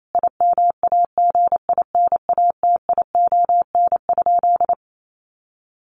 Morse code, heard as a keyed tone, about 28 wpm, 700 Hz. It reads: IMAGINATION?
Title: Sentences from Top 1300 Words 28wpm